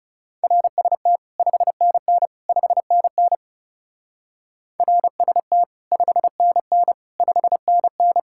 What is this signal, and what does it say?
Morse code, heard as a keyed tone, about 36 wpm, 700 Hz. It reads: RST 5NN 5NN RST 5NN 5NN
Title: Speed-Racing Common QSO Phrases - 30wpm